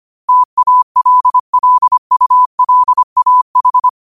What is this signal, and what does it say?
Morse code, heard as a keyed tone, about 25 wpm, 1000 Hz. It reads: TALLULAH
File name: qotdfiles / 25wpm